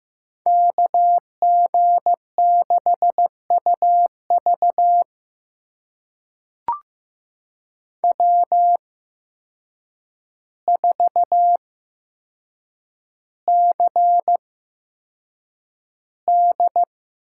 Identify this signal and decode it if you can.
Morse code, heard as a keyed tone, about 15 wpm, spaced as for 7 wpm, 700 Hz. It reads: KG6UV W4CD
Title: USA Call Signs - 5 Characters - 8x Character Spacing - 15wpm